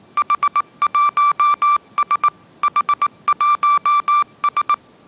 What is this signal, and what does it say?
Morse code, heard as a keyed tone, about 18 wpm, 1170 Hz. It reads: H1S H1S
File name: 20wpm